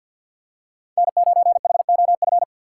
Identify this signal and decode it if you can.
Morse code, heard as a keyed tone, about 49 wpm, 700 Hz. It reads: N9HOF
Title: USA Call Signs - 5 Characters - 6x Character Spacing - 50wpm